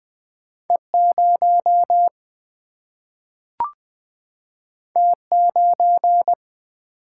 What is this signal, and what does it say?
Morse code, heard as a keyed tone, about 21 wpm, 700 Hz. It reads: E 0 T 9
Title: Single-Letter Single-Number 20wpm